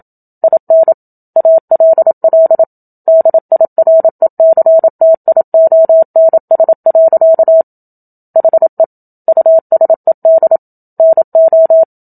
Morse code, heard as a keyed tone, about 27 wpm, 650 Hz. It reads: IN ALL DIRECTIONS. HE USED NO
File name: Book Two / 01-Chapter 1 - 27 WPM